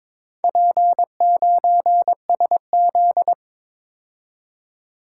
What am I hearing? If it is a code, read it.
Morse code, heard as a keyed tone, about 22 wpm, 700 Hz. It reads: P9SZ
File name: Four Letter-Number Combinations - 22wpm